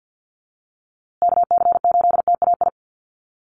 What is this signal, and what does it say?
Morse code, heard as a keyed tone, about 50 wpm, 700 Hz. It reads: KC8TAI
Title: USA Call Signs Repeated 3x 50wpm